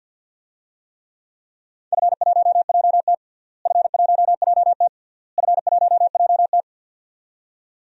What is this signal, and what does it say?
Morse code, heard as a keyed tone, about 52 wpm, 700 Hz. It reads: F1JT F1JT F1JT
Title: DXCC-100 Call Signs - 4 Characters - Repeated 3x - 50wpm